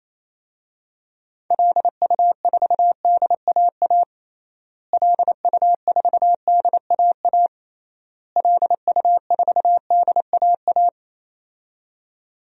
Morse code, heard as a keyed tone, about 28 wpm, 700 Hz. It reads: LU4DAA LU4DAA LU4DAA
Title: DXCC-100 Call Signs - 6 Characters - Repeated 3x - 28wpm